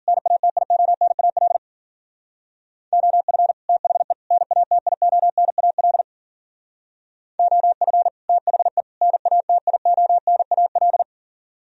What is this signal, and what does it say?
Morse code, heard as a keyed tone, about 50 wpm, 700 Hz. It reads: NATIONAL OF THE NATIONAL OF THE NATIONAL
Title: Speed-Racing - Sets of 3 Words 40wpm